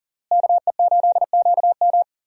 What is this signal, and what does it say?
Morse code, heard as a keyed tone, about 40 wpm, 700 Hz. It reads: KE8QM
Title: USA Call Signs - 5 Characters - Repeated 3x - 40wpm